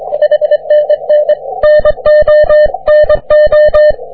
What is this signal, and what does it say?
Morse code, heard as a keyed tone, about 24 wpm, 600 Hz. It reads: HNN NONO
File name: wp / tx5spa 40m cw